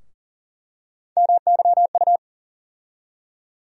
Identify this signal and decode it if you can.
Morse code, heard as a keyed tone, about 40 wpm, 700 Hz. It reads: MYU
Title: USA Call Sign Postfixes - 3 Letters 40wpm